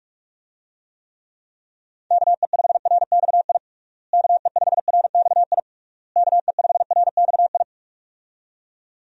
Morse code, heard as a keyed tone, about 41 wpm, 700 Hz. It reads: KE5RXI KE5RXI KE5RXI
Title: USA Call Signs - 6 Characters - Repeated 3x - 45wpm